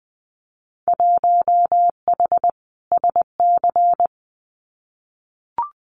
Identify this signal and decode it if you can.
Morse code, heard as a keyed tone, about 20 wpm, 700 Hz. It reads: 1H SC
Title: Winter Field Day Exchanges 20wpm